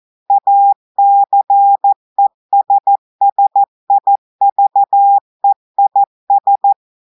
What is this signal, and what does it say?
Morse code, heard as a keyed tone, about 14 wpm, 800 Hz. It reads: ACESSIVEIS
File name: audios / portugais / portugais texte en clair 1 vitesse 14 mots minutes